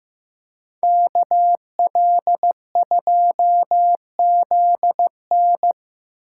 Morse code, heard as a keyed tone, about 18 wpm, 700 Hz. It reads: KL2ZN
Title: Speed Racing - USA 2x2 Call Signs 15wpm